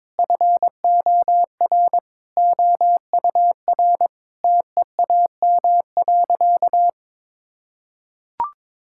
Morse code, heard as a keyed tone, about 22 wpm, 700 Hz. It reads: FOR OUR TEAM.
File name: practice / 22wpm / Sentences with 8-Word Phrases - 22wpm